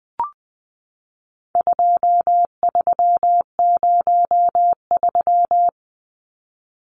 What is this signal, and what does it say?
Morse code, heard as a keyed tone, about 20 wpm, 700 Hz. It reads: 2303
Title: Numbers - 4-Digit 20wpm